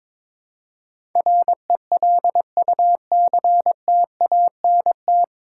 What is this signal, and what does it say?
Morse code, heard as a keyed tone, about 22 wpm, 700 Hz. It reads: RELUCTANT
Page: Morse Code Ninja practice 22wpm